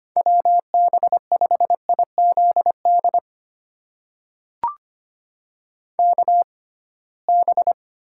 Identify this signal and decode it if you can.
Morse code, heard as a keyed tone, about 25 wpm, spaced as for 19 wpm, 700 Hz. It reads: WB5IZD KB
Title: USA Call Signs - 6 Characters - 6x Character Spacing - 25wpm